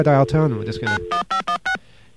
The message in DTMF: 91516